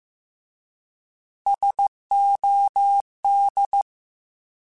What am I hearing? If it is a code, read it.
Morse code, heard as a keyed tone, about 15 wpm, 800 Hz. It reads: SOD
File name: Alphabetical / Alphabetical mp3s / Ss @13